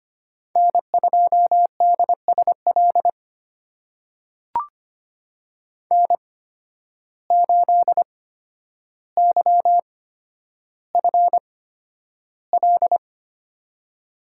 Morse code, heard as a keyed tone, about 25 wpm, spaced as for 11 wpm, 700 Hz. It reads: N2DSL N8YFL
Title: USA Call Signs - 8x Character Spacing - 25wpm